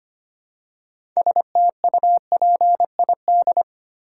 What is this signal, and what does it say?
Morse code, heard as a keyed tone, about 25 wpm, 700 Hz. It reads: STUPID
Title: Six Letter Words - 25wpm